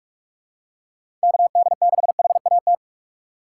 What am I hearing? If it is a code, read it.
Morse code, heard as a keyed tone, about 46 wpm, 700 Hz. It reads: KD6HAT
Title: USA Call Signs - 6 Characters - 3x Character Spacing - 45wpm